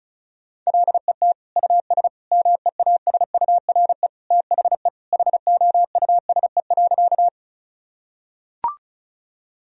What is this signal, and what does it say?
Morse code, heard as a keyed tone, about 34 wpm, 700 Hz. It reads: LET US MEASURE THE HOUSE.
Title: Sentences from Top 1000 Words - Review Part 2 - 35wpm